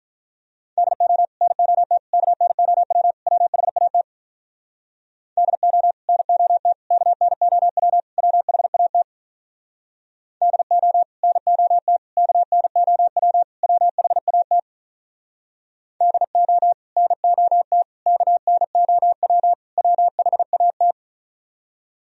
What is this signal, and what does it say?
Morse code, heard as a keyed tone, about 46 wpm, 700 Hz. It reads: DO NOT KNOW WHAT DO NOT KNOW WHAT DO NOT KNOW WHAT DO NOT KNOW WHAT